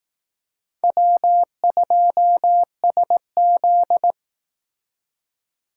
Morse code, heard as a keyed tone, about 18 wpm, 700 Hz. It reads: W2SZ
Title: Speed Racing - USA 1x2 Call Signs 15wpm